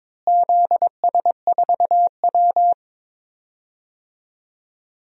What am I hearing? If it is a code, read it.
Morse code, heard as a keyed tone, about 22 wpm, 700 Hz. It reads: ZS4W
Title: DXCC-100 Call Signs - 4 Characters - Repeated 3x - 22wpm